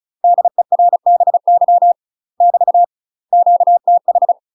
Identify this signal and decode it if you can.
Morse code, heard as a keyed tone, about 33 wpm, 700 Hz. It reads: DERBY <BT> QTH
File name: UK MASTER QSOs TRAINING / 35WPM / UK MASTER QSOs 0007